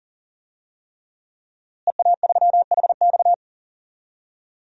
Morse code, heard as a keyed tone, about 40 wpm, 700 Hz. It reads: EA3HX
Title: DXCC-100 Call Signs - 5 Characters - Repeated 3x - 40wpm